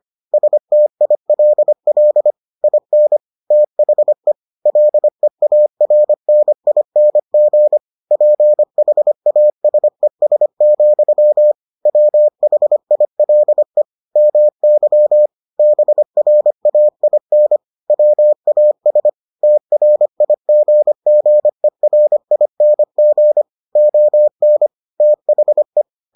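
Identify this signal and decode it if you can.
Morse code, heard as a keyed tone, about 25 wpm, 600 Hz. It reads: STILL IN THE LEARNING PHASES, WHILE MY BRAIN WAS TRIGGERING ON THE
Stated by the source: Morse edition of the Foundations of Amateur Radio episodes.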